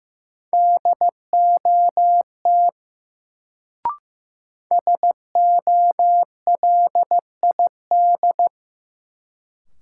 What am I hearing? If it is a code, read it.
Morse code, heard as a keyed tone, about 15 wpm, 700 Hz. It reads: DOT SOLID
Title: Sentences from Top 1500 Words - New Word Drill - 15wpm